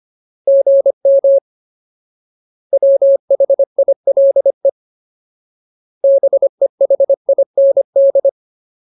Morse code, heard as a keyed tone, about 25 wpm, 550 Hz. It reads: GM WHILE BEHIND
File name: Ultimate Word List / 25WPM Ultimate